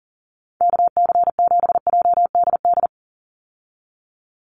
Morse code, heard as a keyed tone, about 40 wpm, 700 Hz. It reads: KC7JDD
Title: USA Call Signs 40wpm